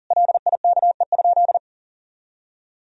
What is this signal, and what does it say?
Morse code, heard as a keyed tone, about 40 wpm, 700 Hz. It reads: LIKE?
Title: Sentences from Top 100 Words and Abbreviations 40wpm